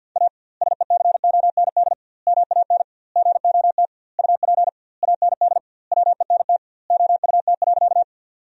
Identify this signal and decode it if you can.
Morse code, heard as a keyed tone, about 52 wpm, 700 Hz. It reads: A SECOND MAN GOT UP AND WENT OUT.
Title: 2x Word-Spacing - Sentences from Top 300 Words - 50wpm